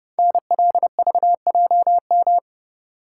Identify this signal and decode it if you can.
Morse code, heard as a keyed tone, about 30 wpm, 700 Hz. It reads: NLVJM
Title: Code Groups - 5 Letters - 30wpm